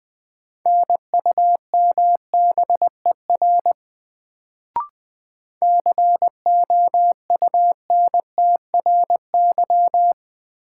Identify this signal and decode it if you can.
Morse code, heard as a keyed tone, about 20 wpm, 700 Hz. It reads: NUMBER COUNTRY
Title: Sets of 2 Words - New Word Drill - 20wpm